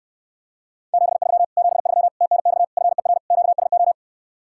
Morse code, heard as a keyed tone, about 71 wpm, 700 Hz. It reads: 73 73 TNX FER QSO